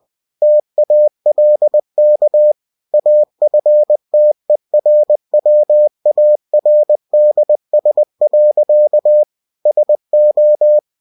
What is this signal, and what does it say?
Morse code, heard as a keyed tone, about 20 wpm, 600 Hz. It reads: TALK AFTERWARDS. SO
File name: cw-books / winnie-the-pooh / winnie-20wpm-page-0107